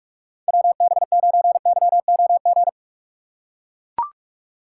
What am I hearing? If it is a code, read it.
Morse code, heard as a keyed tone, about 42 wpm, 700 Hz. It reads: WB9YOG
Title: USA Call Signs - 6 Characters - 8x Character Spacing - 45wpm